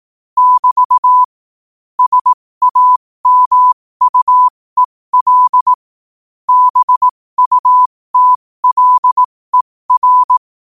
Morse code, heard as a keyed tone, about 18 wpm, spaced as for 15 wpm, 1000 Hz. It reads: <BT> SAMUEL BUTLER
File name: qotdfiles / 15wpm